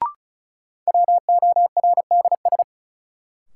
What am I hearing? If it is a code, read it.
Morse code, heard as a keyed tone, about 36 wpm, 700 Hz. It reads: WORDS
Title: Sentences from Top 1000 Words - Review Part 6 - New Word Drill - 35wpm